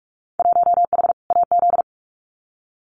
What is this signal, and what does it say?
Morse code, heard as a keyed tone, about 44 wpm, 700 Hz. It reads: 1H AZ
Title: Winter Field Day Exchanges 45wpm